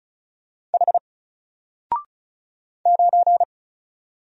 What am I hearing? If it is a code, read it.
Morse code, heard as a keyed tone, about 33 wpm, 700 Hz. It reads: H 9
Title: Single Letter-Number 35wpm